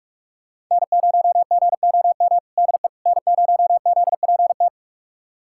Morse code, heard as a keyed tone, about 46 wpm, 700 Hz. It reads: N0GOM DE N0ZPT
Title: USA Callsign DE Callsign - 5 Characters - 45wpm